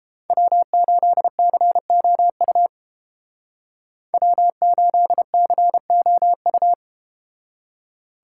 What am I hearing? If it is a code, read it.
Morse code, heard as a keyed tone, about 31 wpm, 700 Hz. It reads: W8COU W8COU